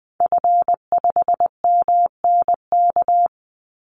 Morse code, heard as a keyed tone, about 20 wpm, 700 Hz. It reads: F5MNK
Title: Call Signs - 20wpm